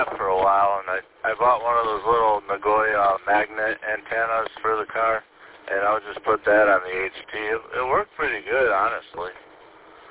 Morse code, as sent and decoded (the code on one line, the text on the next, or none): none